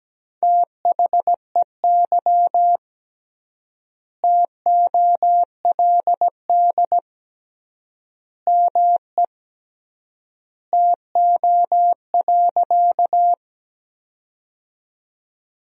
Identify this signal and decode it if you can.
Morse code, heard as a keyed tone, about 17 wpm, 700 Hz. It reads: THEY TOLD ME TO.
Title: 3x Word-Spacing - Sentences from Top 400 Words - 17wpm